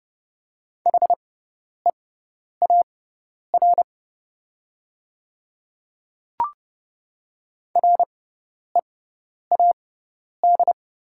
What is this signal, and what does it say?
Morse code, heard as a keyed tone, about 30 wpm, spaced as for 10 wpm, 700 Hz. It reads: HEAR READ
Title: 6x Character Spacing - Top 300 Words - 30wpm